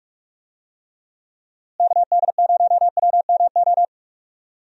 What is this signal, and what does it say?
Morse code, heard as a keyed tone, about 46 wpm, 700 Hz. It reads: KD0WMO
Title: USA Call Signs - 6 Characters - Repeated 3x - 45wpm